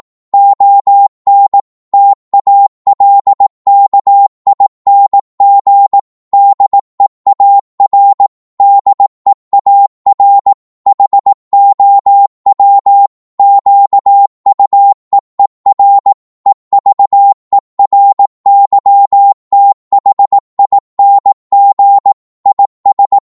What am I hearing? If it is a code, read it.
Morse code, heard as a keyed tone, about 18 wpm, 800 Hz. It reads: ON TALKING DEAR DEAR HOW QUEER EVERYTHING IS
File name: Morse / psg-18wpm